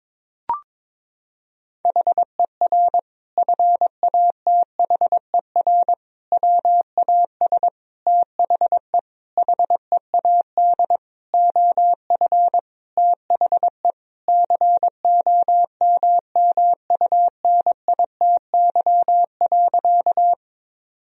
Morse code, heard as a keyed tone, about 22 wpm, 700 Hz. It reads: HER FATHER WAS THE HEAD OF THE COMMUNITY.
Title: Sentences from Top 1000 Words - Review Part 4 - 22wpm